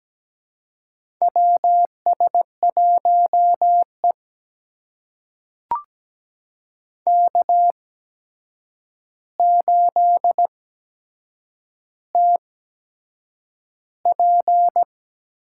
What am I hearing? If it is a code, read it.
Morse code, heard as a keyed tone, about 17 wpm, spaced as for 7 wpm, 700 Hz. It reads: WS1E K8TP
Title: USA Call Signs - 4 Characters - 8x Character Spacing - 17wpm